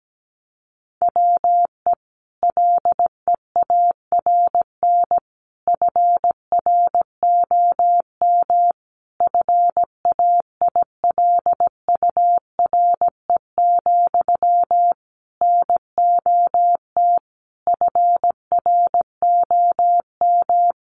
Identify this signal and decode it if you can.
Morse code, heard as a keyed tone, about 17 wpm, 700 Hz. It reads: WE LEARN FROM FAILURE, NOT FROM
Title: Sentences from Top 900 Words 17wpm